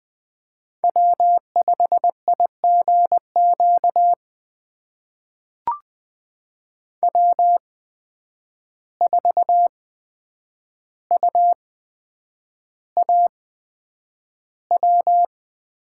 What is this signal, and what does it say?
Morse code, heard as a keyed tone, about 21 wpm, spaced as for 9 wpm, 700 Hz. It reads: W5IGQ W4UAW
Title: USA Call Signs - 8x Character Spacing - 20wpm